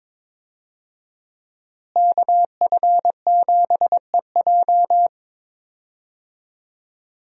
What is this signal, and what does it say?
Morse code, heard as a keyed tone, about 21 wpm, 700 Hz. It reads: KF7EJ
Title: USA Call Signs - 5 Characters - Repeated 3x - 22wpm